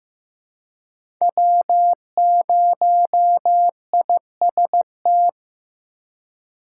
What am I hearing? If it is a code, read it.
Morse code, heard as a keyed tone, about 16 wpm, 700 Hz. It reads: W0IST
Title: USA Call Signs - 5 Characters - 4x Character Spacing - 15wpm